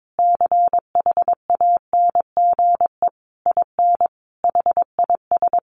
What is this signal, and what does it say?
Morse code, heard as a keyed tone, about 22 wpm, 700 Hz. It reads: CHANGE IN HIS
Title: Sentences from Top 1100 Words 22wpm